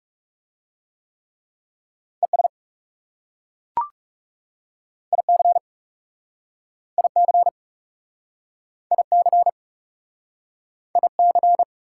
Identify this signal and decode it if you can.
Morse code, heard as a keyed tone, about 41 wpm, 700 Hz. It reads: ES IC IC IC IC